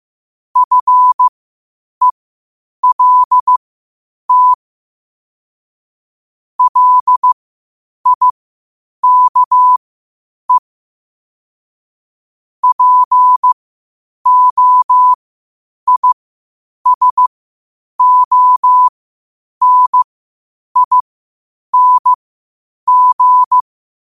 Morse code, heard as a keyed tone, about 15 wpm, spaced as for 8 wpm, 1000 Hz. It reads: FELT LIKE POISONING
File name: qotdfiles / 8wpm